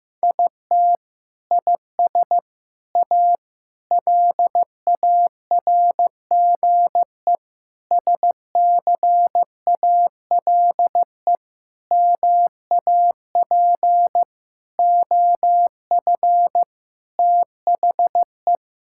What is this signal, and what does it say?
Morse code, heard as a keyed tone, about 15 wpm, 700 Hz. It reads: IT IS A LARGE SCALE MAP OF THE
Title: Sentences from Top 1000 Words - Review Part 5 - 15wpm